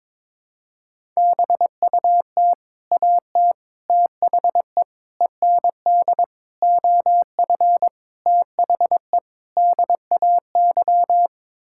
Morse code, heard as a keyed tone, about 22 wpm, 700 Hz. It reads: BUT AT THE END OF THE DAY
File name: Sets of 7 Words - 22wpm